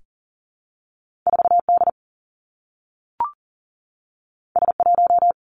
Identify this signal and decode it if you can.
Morse code, heard as a keyed tone, about 42 wpm, 700 Hz. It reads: 4 D S 1
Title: Two Challenging Letter-Number Combinations 40wpm